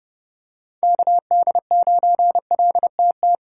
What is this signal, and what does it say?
Morse code, heard as a keyed tone, about 30 wpm, 700 Hz. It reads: KD9LTT